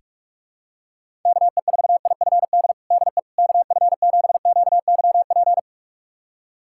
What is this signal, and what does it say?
Morse code, heard as a keyed tone, about 46 wpm, 700 Hz. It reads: KE4IFD DE KF7QYP
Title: USA Callsign DE Callsign - 6 Characters - 45wpm